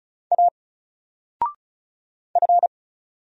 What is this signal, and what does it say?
Morse code, heard as a keyed tone, about 35 wpm, 700 Hz. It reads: A F
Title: Single Letter-Number 35wpm